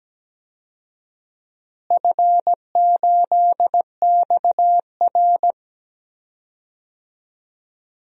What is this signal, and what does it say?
Morse code, heard as a keyed tone, about 17 wpm, 700 Hz. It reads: F8XR
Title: DXCC-100 Call Signs - 4 Characters - Repeated 2x - 17wpm